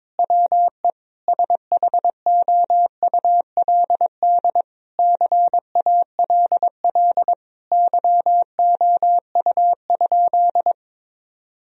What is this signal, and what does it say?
Morse code, heard as a keyed tone, about 22 wpm, 700 Hz. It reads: WE SHOULD CALL YOU?
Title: Sentences from Top 1000 Words - Review Part 3 - 22wpm